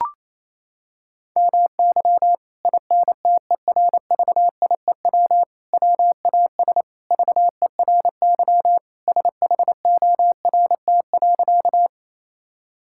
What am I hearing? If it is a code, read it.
Morse code, heard as a keyed tone, about 28 wpm, 700 Hz. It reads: MY INTERVIEW WAS VERY SHORT.
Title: Sentences from Top 1600 Words 28wpm